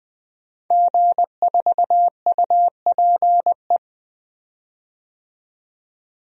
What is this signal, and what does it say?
Morse code, heard as a keyed tone, about 21 wpm, 700 Hz. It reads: G4UPE
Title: DXCC-100 Call Signs - 5 Characters - 20wpm